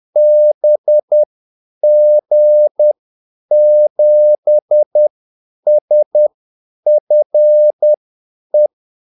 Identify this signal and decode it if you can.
Morse code, heard as a keyed tone, about 10 wpm, spaced as for 8 wpm, 600 Hz. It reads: BG7SFE
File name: Callsigns-10-8wpm